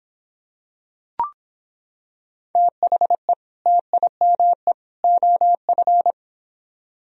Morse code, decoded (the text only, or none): THE TIME OF